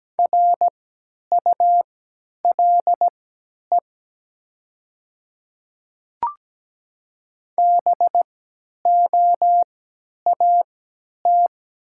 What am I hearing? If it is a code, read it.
Morse code, heard as a keyed tone, about 17 wpm, spaced as for 10 wpm, 700 Hz. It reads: RULE BOAT
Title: 3x Character Spacing - Top 500 Words - 17wpm